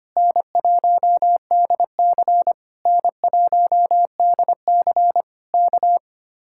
Morse code, heard as a keyed tone, about 24 wpm, 700 Hz. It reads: N1DC N1DC K
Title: Speed-Racing Common QSO Phrases - 25wpm